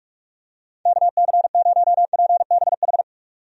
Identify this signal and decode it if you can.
Morse code, heard as a keyed tone, about 45 wpm, 700 Hz. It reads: KC0PBH
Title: USA Call Signs - 6 Characters - 45wpm